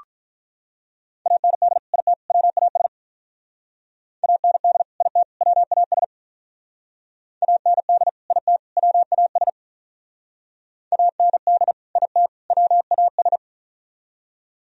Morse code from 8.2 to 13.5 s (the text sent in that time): IT WAS AND IT WAS